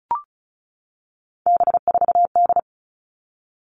Morse code, heard as a keyed tone, about 34 wpm, 700 Hz. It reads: B4D
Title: All Cut Numbers - 3-Digit 35wpm